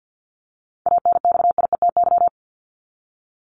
Morse code, heard as a keyed tone, about 48 wpm, 700 Hz. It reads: ANXIETY
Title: Top 1501-1750 Words 50wpm